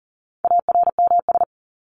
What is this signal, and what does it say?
Morse code, heard as a keyed tone, about 40 wpm, 700 Hz. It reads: ARMS
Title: ES:S Suffix - Top 100 Words with Base Words 40wpm